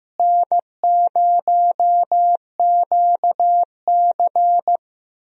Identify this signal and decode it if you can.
Morse code, heard as a keyed tone, about 15 wpm, 700 Hz. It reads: N0QC
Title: USA Call Signs - 4 Characters - Repeated 3x - 15wpm